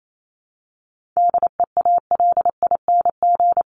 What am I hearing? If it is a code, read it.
Morse code, heard as a keyed tone, about 28 wpm, 700 Hz. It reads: DEALING
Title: ING Suffix - Top 100 Words with Base Words 28wpm